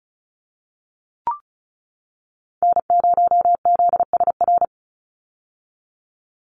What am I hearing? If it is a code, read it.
Morse code, heard as a keyed tone, about 34 wpm, 700 Hz. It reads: N0ZSR
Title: USA Call Signs 35wpm